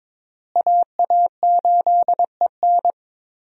Morse code, heard as a keyed tone, about 22 wpm, 700 Hz. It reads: AA8EN
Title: USA Call Signs - 5 Characters - Repeated 2x - 22wpm